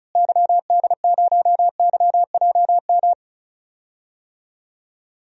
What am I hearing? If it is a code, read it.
Morse code, heard as a keyed tone, about 34 wpm, 700 Hz. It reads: YD0YJM
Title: DXCC-100 Call Signs - 6 Characters - Repeated 3x - 35wpm